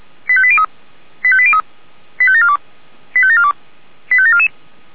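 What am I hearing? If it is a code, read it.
Five-tone selective call (ZVEI), heard as quick runs of five tones, tones of about 70 ms. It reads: 85803 85803 85742 85742 85744
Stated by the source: ZVEI five-tone selective call received with an SDR.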